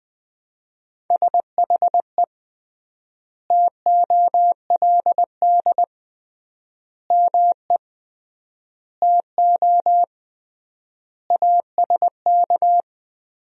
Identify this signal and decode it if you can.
Morse code, heard as a keyed tone, about 20 wpm, 700 Hz. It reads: SHE TOLD ME TO ASK
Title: 3x Word-Spacing - Sentences from Top 400 Words - 20wpm